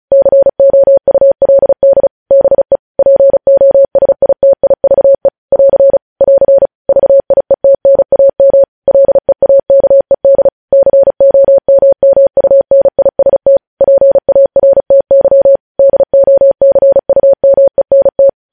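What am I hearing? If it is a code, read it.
Morse code, heard as a keyed tone, about 36 wpm, 550 Hz. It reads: COULD BE POSITIVE <AR> <AR> VIETNAM LEAKED COMMUNIST PARTY DOCUMENT
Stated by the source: Morse code practice sending news headlines at 35 wpm.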